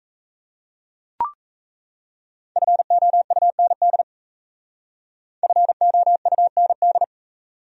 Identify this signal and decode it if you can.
Morse code, heard as a keyed tone, about 37 wpm, 700 Hz. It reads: FOUND FOUND